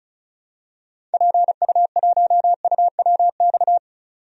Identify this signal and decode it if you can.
Morse code, heard as a keyed tone, about 36 wpm, 700 Hz. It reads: PU1UWX
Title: DXCC-100 Call Signs - 6 Characters - 35wpm